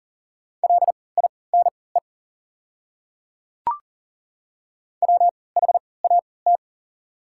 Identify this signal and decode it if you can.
Morse code, heard as a keyed tone, about 40 wpm, spaced as for 23 wpm, 690 Hz. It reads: LINE WHAT
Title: 3x Character Spacing - Top 200 Words - 40wpm